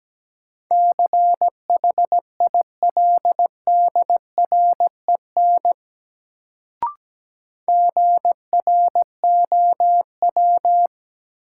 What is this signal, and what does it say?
Morse code, heard as a keyed tone, about 17 wpm, 700 Hz. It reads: CHILDREN GROW
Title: Sentences from Top 300 Words - New Word Drill - 17wpm